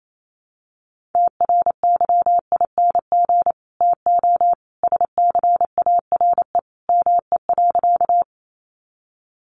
Morse code, heard as a keyed tone, about 28 wpm, 700 Hz. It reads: TRYING TO SCARE ME.
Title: A Sound Of Thunder 28wpm